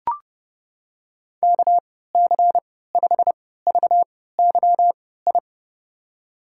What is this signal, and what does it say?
Morse code, heard as a keyed tone, about 30 wpm, spaced as for 17 wpm, 700 Hz. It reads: KC5VYI
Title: USA Call Signs - 3x Character Spacing - 30wpm